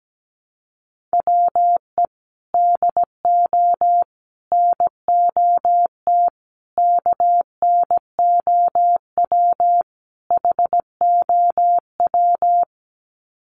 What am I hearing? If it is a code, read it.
Morse code, heard as a keyed tone, about 17 wpm, 700 Hz. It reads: WE DO NOT KNOW HOW
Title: Sets of 5 Words Encore 17wpm